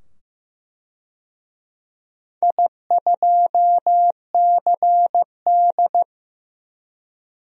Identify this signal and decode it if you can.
Morse code, heard as a keyed tone, about 15 wpm, 700 Hz. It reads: I2CD
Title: DXCC-100 Call Signs - 4 Characters - 15wpm